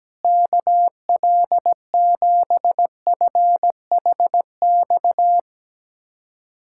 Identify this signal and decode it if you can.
Morse code, heard as a keyed tone, about 17 wpm, 700 Hz. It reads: KL7FHX
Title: USA Callsign DE Callsign - 6 Characters - 17wpm